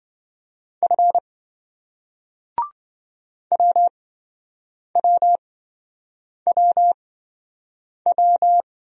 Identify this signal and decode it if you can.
Morse code, heard as a keyed tone, about 27 wpm, spaced as for 7 wpm, 700 Hz. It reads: F WWWW